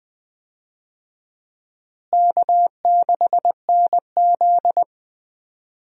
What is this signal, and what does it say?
Morse code, heard as a keyed tone, about 20 wpm, 700 Hz. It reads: K6NZ
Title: USA Call Signs - 4 Characters - Repeated 3x - 20wpm